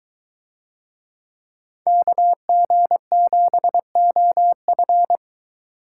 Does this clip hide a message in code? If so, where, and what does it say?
Yes: Morse code from 1.9 to 5.2 s, heard as a keyed tone, about 23 wpm, 700 Hz. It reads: KG7OF